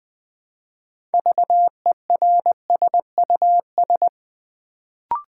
Morse code, heard as a keyed tone, about 20 wpm, 700 Hz. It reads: VERSUS
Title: Build A Word - 2 Letters At A Time - 6-Letter Words - 20wpm